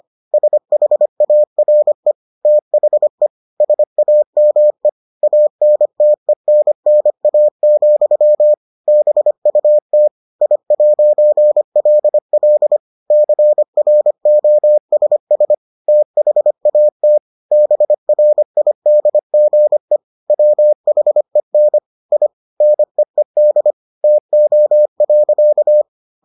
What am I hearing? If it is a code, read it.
Morse code, heard as a keyed tone, about 25 wpm, 600 Hz. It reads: SHARE THE SAME ANTENNA, BUT I'LL CROSS THAT BRIDGE WHEN I NEED TO.